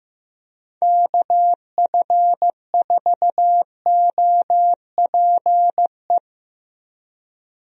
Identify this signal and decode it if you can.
Morse code, heard as a keyed tone, about 15 wpm, 700 Hz. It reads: KF4OPE
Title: USA Call Signs - 6 Characters - Repeated 2x - 15wpm